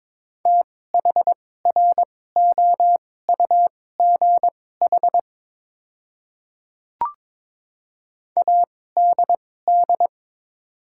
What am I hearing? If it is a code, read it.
Morse code, heard as a keyed tone, about 22 wpm, spaced as for 16 wpm, 700 Hz. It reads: THROUGH ADD
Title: 2x Character Spacing - Top 500 Words - 22wpm